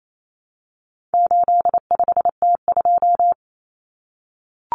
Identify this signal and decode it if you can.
Morse code, heard as a keyed tone, about 28 wpm, 700 Hz. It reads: 85T2
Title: Common Cut Numbers - 4-Digit 28wpm